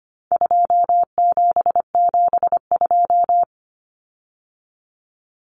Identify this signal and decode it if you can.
Morse code, heard as a keyed tone, about 25 wpm, 700 Hz. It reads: 2772
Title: Common Cut Numbers - 4-Digit 25wpm